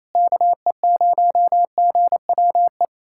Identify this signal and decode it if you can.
Morse code, heard as a keyed tone, about 28 wpm, 700 Hz. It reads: KE0GWE
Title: USA Callsign DE Callsign 28wpm